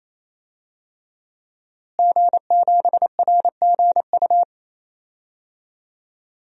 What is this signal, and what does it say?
Morse code, heard as a keyed tone, about 28 wpm, 700 Hz. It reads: G7RGU
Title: DXCC-100 Call Signs - 5 Characters - Repeated 3x - 28wpm